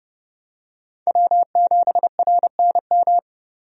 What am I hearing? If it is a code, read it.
Morse code, heard as a keyed tone, about 30 wpm, 700 Hz. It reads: W7RNM
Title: USA Call Signs - 5 Characters - 30wpm